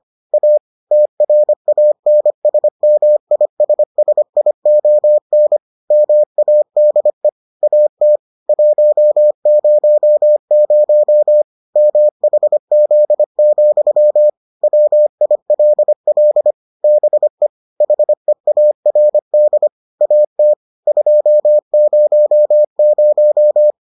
Morse code, heard as a keyed tone, about 25 wpm, 600 Hz. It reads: A TRANSMISSION MADE AT 100 MHZ, WILL BE HEARD AT 200